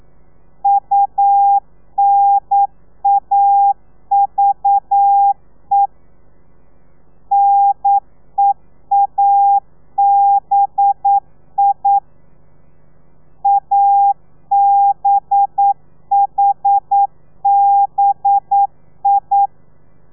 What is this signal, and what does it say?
Morse code, heard as a keyed tone, about 9 wpm, 800 Hz. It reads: UNAVE NEABI ABHBI
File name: telegrafia / i2tfj / Lez4 40